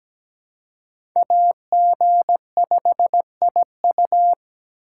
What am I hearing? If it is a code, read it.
Morse code, heard as a keyed tone, about 17 wpm, 700 Hz. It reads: AG5IU
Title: USA Call Signs - 4x Character Spacing - 17wpm